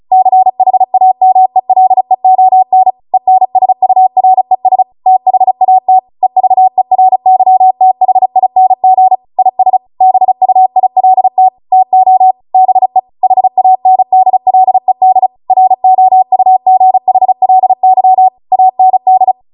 Morse code, heard as a keyed tone, about 36 wpm, 750 Hz. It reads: CHAMELEON ENSURES THAT EVERYTHING IS BUILT TO BE HANDLED ROUGHLY AND